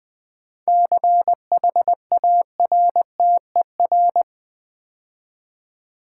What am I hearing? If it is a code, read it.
Morse code, heard as a keyed tone, about 20 wpm, 700 Hz. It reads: CHARTER
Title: 7 Letter Words - First 3 Letters Given - 20wpm